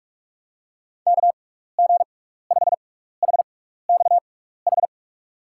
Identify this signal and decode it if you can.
Morse code, heard as a keyed tone, about 46 wpm, spaced as for 16 wpm, 700 Hz. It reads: KG5HXH
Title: USA Call Signs - 6x Character Spacing - 45wpm